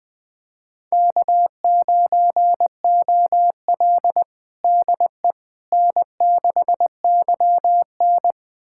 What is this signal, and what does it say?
Morse code, heard as a keyed tone, about 21 wpm, 700 Hz. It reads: K9OL DE N6YN
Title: USA Callsign DE Callsign - 4 Characters - 20wpm